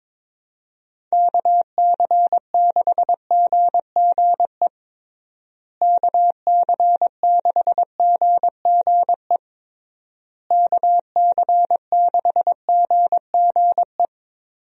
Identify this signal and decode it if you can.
Morse code, heard as a keyed tone, about 21 wpm, 700 Hz. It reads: KC6GGE KC6GGE KC6GGE
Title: USA Call Signs - 6 Characters - Repeated 3x - 22wpm